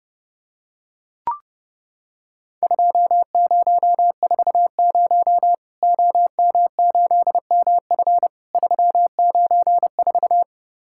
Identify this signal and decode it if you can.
Morse code, heard as a keyed tone, about 32 wpm, 700 Hz. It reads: 2040 OM8MF 394
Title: Worked All Europe DX Contest - QTC Exchange - 30wpm